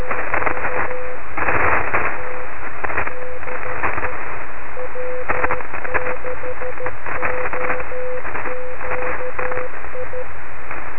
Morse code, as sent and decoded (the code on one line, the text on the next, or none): none